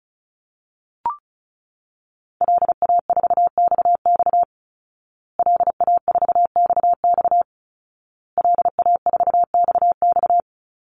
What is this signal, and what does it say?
Morse code, heard as a keyed tone, about 34 wpm, 700 Hz. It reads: LA4XX LA4XX LA4XX
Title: Call Signs Repeated 2x - 35wpm